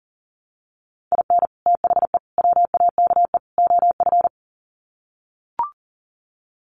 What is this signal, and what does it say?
Morse code, heard as a keyed tone, about 40 wpm, 700 Hz. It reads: IN THE WAKE OF
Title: Sets of 4 Words 40wpm